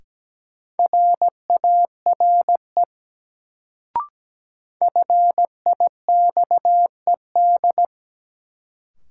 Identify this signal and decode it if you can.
Morse code, heard as a keyed tone, about 17 wpm, 700 Hz. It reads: RARE FIXED
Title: Sentences from Top 1100 Words - New Word Drill - 17wpm